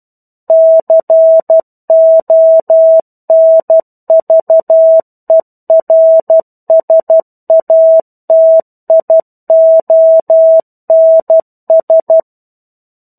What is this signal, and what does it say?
Morse code, heard as a keyed tone, about 12 wpm, 650 Hz. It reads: CONVERSATIONS
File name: 01-Chapter-01 12 WPM